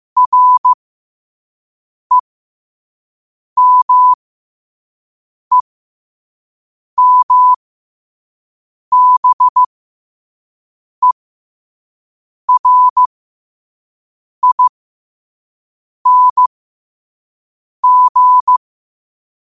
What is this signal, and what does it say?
Morse code, heard as a keyed tone, about 15 wpm, spaced as for 5 wpm, 1000 Hz. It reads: REMEMBERING
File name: qotdfiles / 5wpm